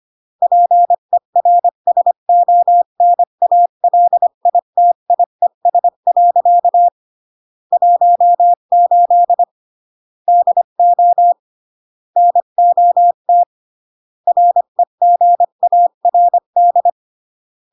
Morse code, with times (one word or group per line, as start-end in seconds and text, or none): 0.4-6.9 PERSONALITIES.
7.7-9.4 18
10.3-11.3 DO
12.2-13.4 NOT
14.3-16.9 REGARD